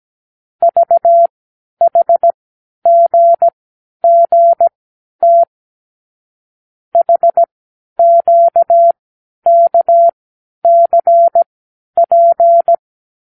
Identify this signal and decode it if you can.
Morse code, heard as a keyed tone, about 17 wpm, spaced as for 10 wpm, 680 Hz. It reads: VHGGT HQKCP
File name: k7qo / Lesson 36 (A-X)